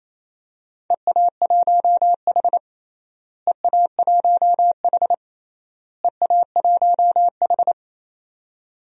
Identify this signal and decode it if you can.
Morse code, heard as a keyed tone, about 28 wpm, 700 Hz. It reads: EA1H EA1H EA1H
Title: DXCC-100 Call Signs - 4 Characters - Repeated 3x - 28wpm